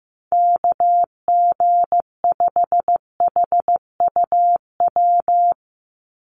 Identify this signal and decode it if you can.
Morse code, heard as a keyed tone, about 15 wpm, 700 Hz. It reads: KG5HUW
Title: USA Call Signs 15wpm